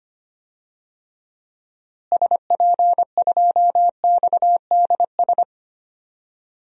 Morse code, heard as a keyed tone, about 25 wpm, 700 Hz. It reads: SP2XDS
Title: DXCC-100 Call Signs - 6 Characters - Repeated 2x - 25wpm